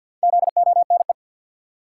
Morse code, heard as a keyed tone, about 52 wpm, 700 Hz. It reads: ZONE
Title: Four Letter Words - 50wpm